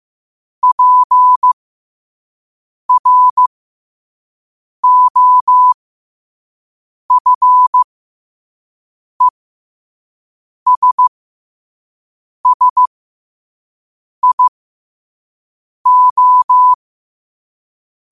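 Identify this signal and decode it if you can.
Morse code, heard as a keyed tone, about 15 wpm, spaced as for 5 wpm, 1000 Hz. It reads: PROFESSIO
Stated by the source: Morse code sent at 5 wpm.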